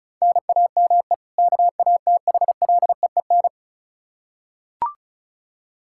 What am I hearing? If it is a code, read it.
Morse code, heard as a keyed tone, about 36 wpm, 700 Hz. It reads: NAME KATHLEEN
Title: Practice QSO - Exchange 1 - Beginning - 35wpm